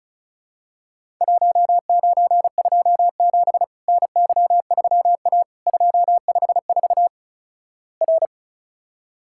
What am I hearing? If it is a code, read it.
Morse code, heard as a keyed tone, about 36 wpm, 700 Hz. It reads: 1927 NY3A 254 R
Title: Worked All Europe DX Contest - QTC Exchange - 35wpm